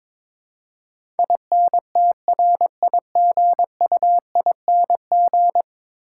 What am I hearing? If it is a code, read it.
Morse code, heard as a keyed tone, about 22 wpm, 700 Hz. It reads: INTRIGUING
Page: morse-code-ninja-practice-22wpm